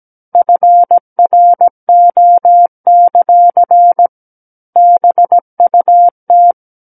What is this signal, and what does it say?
Morse code, heard as a keyed tone, about 17 wpm, 700 Hz. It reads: FRO; BUT
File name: Book One / 08-Chapter 8 - 17 WPM